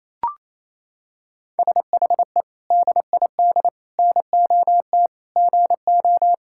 Morse code, heard as a keyed tone, about 28 wpm, 700 Hz. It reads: SHE DID NOT GO
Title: Sentences from Top 1000 Words - Review Part 6 - 28wpm